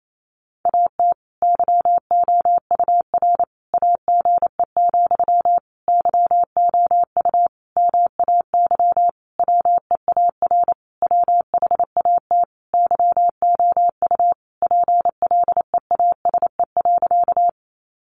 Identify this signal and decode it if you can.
Morse code, heard as a keyed tone, about 28 wpm, 700 Hz. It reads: AT YOUR AGE, YOU MAY WEAR WHAT YOU PLEASE.
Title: Sentences from Top 1500 Words 28wpm